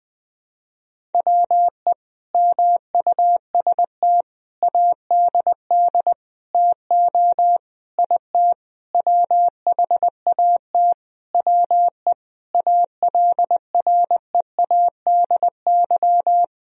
Morse code, heard as a keyed tone, about 20 wpm, 700 Hz. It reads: WE MUST ADD TO IT WHAT WE ALREADY
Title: Sentences from Top 1000 Words - Review Part 1 - 20wpm